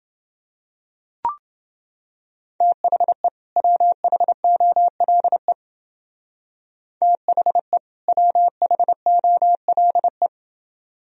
Morse code, heard as a keyed tone, about 31 wpm, 700 Hz. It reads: THE WHOLE THE WHOLE